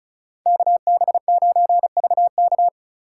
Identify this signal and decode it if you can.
Morse code, heard as a keyed tone, about 44 wpm, 700 Hz. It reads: KB9VK